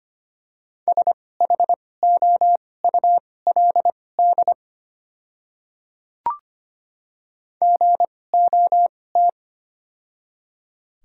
Morse code, heard as a keyed tone, about 25 wpm, spaced as for 18 wpm, 700 Hz. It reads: SHOULD GOT
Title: 2x Character Spacing - Top 300 Words - 25wpm